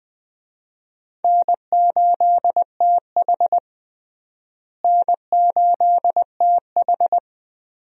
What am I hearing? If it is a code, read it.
Morse code, heard as a keyed tone, about 21 wpm, 700 Hz. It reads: N8TH N8TH
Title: USA Call Signs - 4 Characters - Repeated 3x - 20wpm